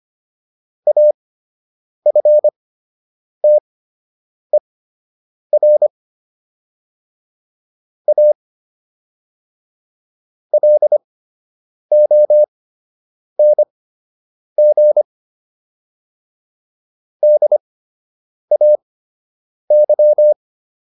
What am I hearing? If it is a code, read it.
Morse code, heard as a keyed tone, about 25 wpm, spaced as for 8 wpm, 600 Hz. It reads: AFTER A LONG DAY